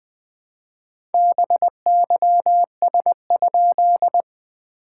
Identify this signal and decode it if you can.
Morse code, heard as a keyed tone, about 21 wpm, 690 Hz. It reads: BYS?
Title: SOTA QSO Practice 20wpm